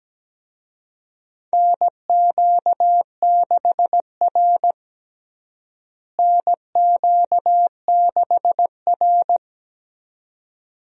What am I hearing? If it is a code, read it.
Morse code, heard as a keyed tone, about 17 wpm, 700 Hz. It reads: NQ6R NQ6R
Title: USA Call Signs - 4 Characters - Repeated 2x - 17wpm